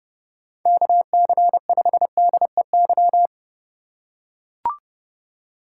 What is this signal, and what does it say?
Morse code, heard as a keyed tone, about 29 wpm, 700 Hz. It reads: KC5DEY
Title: USA Call Signs - 6 Characters - 4x Character Spacing - 30wpm